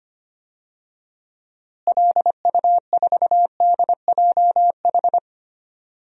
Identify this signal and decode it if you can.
Morse code, heard as a keyed tone, about 25 wpm, 700 Hz. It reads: LU4DJH
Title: DXCC-100 Call Signs - 6 Characters - Repeated 2x - 25wpm